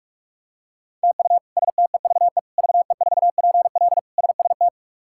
Morse code, heard as a keyed tone, about 46 wpm, 700 Hz. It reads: TU STEVE VE4PL SST
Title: K1USN SST QSO Practice - 45wpm